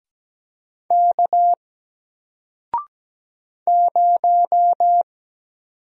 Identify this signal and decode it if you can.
Morse code, heard as a keyed tone, about 17 wpm, 700 Hz. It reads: K 0
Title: Single Letter-Number Alt 17wpm